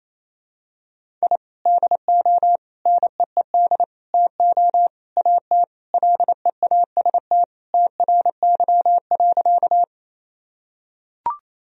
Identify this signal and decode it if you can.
Morse code, heard as a keyed tone, about 28 wpm, 700 Hz. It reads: I DO NEED TO AT LEAST TRY.
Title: Sentences from Top 1000 Words - Review Part 1 - 28wpm